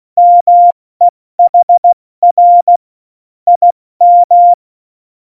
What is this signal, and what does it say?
Morse code, heard as a keyed tone, about 16 wpm, 700 Hz. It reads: MEHR IM
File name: Morsekurs / Witze / Witz 17. November 2020, 70 F80